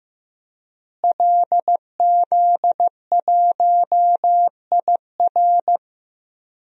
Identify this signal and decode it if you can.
Morse code, heard as a keyed tone, about 15 wpm, 700 Hz. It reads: LZ1IR
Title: DXCC-100 Call Signs - 5 Characters - Repeated 3x - 15wpm